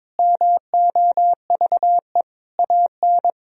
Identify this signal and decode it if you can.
Morse code, heard as a keyed tone, about 22 wpm, 700 Hz. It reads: MOVE AN
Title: Sentences from Top 1000 Words - Review Part 3 - 22wpm